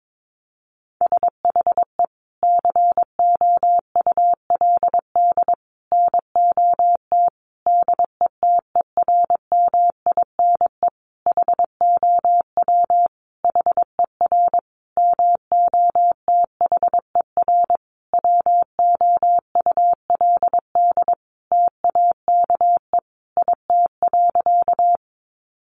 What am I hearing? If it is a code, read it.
Morse code, heard as a keyed tone, about 22 wpm, 700 Hz. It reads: SHE COULD NOT DETERMINE HOW HER MOTHER WOULD TAKE IT.
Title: Sentences from Top 700 Words 22wpm